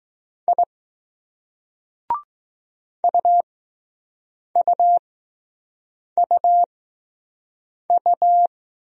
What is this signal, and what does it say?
Morse code, heard as a keyed tone, about 20 wpm, spaced as for 6 wpm, 700 Hz. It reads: I UUUU